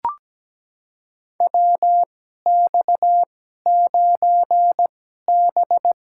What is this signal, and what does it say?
Morse code, heard as a keyed tone, about 17 wpm, spaced as for 12 wpm, 700 Hz. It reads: WX9B
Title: USA Call Signs - 4 Characters - 2x Character Spacing - 17wpm